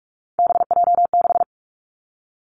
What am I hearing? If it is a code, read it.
Morse code, heard as a keyed tone, about 46 wpm, 700 Hz. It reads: BJ6
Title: Three Challenging Letter-Number Combinations 45wpm